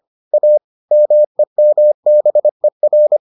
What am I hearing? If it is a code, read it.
Morse code, heard as a keyed tone, about 25 wpm, 600 Hz. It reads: A MEMBER